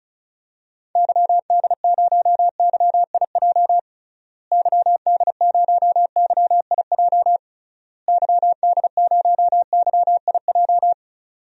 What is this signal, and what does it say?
Morse code, heard as a keyed tone, about 36 wpm, 700 Hz. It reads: YD0YIJ YD0YIJ YD0YIJ
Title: DXCC-100 Call Signs - 6 Characters - Repeated 3x - 35wpm